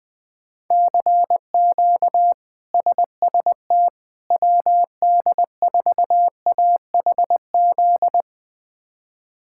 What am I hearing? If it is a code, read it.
Morse code, heard as a keyed tone, about 20 wpm, 700 Hz. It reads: CQ SST WD4AHZ
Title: K1USN SST QSO Practice - 20wpm